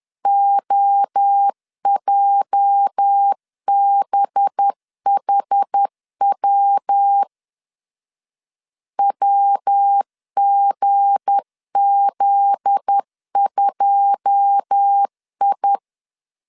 Morse code, heard as a keyed tone, about 10 wpm, 800 Hz. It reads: OJBHW WGZ2I